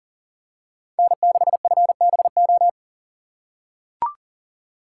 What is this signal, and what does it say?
Morse code, heard as a keyed tone, about 37 wpm, 700 Hz. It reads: N6FBO
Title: USA Call Signs - 4x Character Spacing - 40wpm